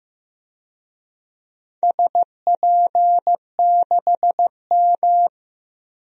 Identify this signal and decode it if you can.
Morse code, heard as a keyed tone, about 15 wpm, 700 Hz. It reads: SP6M
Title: DXCC-100 Call Signs - 4 Characters - Repeated 2x - 15wpm